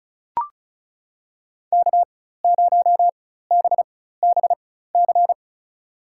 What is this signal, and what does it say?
Morse code, heard as a keyed tone, about 36 wpm, spaced as for 16 wpm, 700 Hz. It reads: K0BBC
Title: USA Call Signs - 5 Characters - 4x Character Spacing - 35wpm